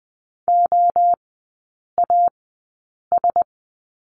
Morse code, heard as a keyed tone, about 20 wpm, spaced as for 8 wpm, 700 Hz. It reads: OAS
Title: 127 Lesson 21e - Number 6 - Three Characters 20wpm